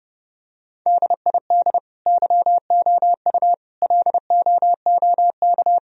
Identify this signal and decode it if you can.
Morse code, heard as a keyed tone, about 30 wpm, 700 Hz. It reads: DID YOU LOOK
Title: Sentences from Top 1000 Words - Review Part 4 - 30wpm